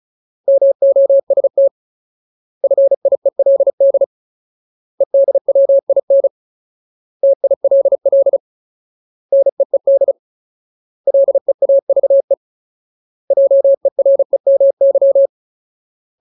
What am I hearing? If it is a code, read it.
Morse code, heard as a keyed tone, about 34 wpm, 550 Hz. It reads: MOST FIELD EDWIN TILL NEED LEAVE JEREMY
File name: Ultimate Word List / 35WPM Ultimate